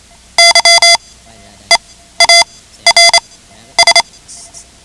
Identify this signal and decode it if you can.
Morse code, heard as a keyed tone, about 28 wpm, spaced as for 12 wpm, 2050 Hz. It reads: YEARS
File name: ham radio / CodeQuick / Code Quick 10